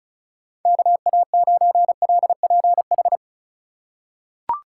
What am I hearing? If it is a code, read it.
Morse code, heard as a keyed tone, about 36 wpm, 700 Hz. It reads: KA9LPH
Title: USA Call Signs - 4x Character Spacing - 35wpm